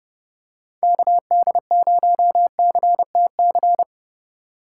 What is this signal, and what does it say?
Morse code, heard as a keyed tone, about 30 wpm, 700 Hz. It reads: KD0CTC